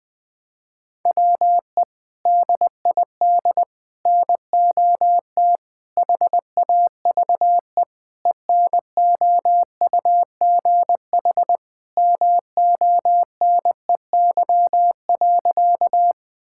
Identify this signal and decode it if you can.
Morse code, heard as a keyed tone, about 20 wpm, 700 Hz. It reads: WE DID NOT HAVE ENOUGH MONEY.
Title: Sentences with 4-Word Phrases - Spoken Beginning - 20wpm